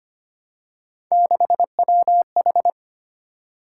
Morse code, heard as a keyed tone, about 25 wpm, 700 Hz. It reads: 6WH
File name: Three Letter-Number Combinations - 25wpm